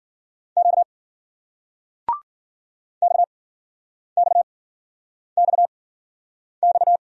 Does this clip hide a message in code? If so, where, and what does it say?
Yes: Morse code from 0.6 to 7.0 s, heard as a keyed tone, about 58 wpm, spaced as for 9 wpm, 700 Hz. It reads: Q XXXX